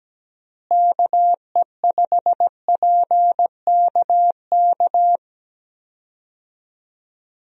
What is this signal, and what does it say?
Morse code, heard as a keyed tone, about 17 wpm, 700 Hz. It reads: KE5PKK
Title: USA Call Signs - 6 Characters - Repeated 2x - 17wpm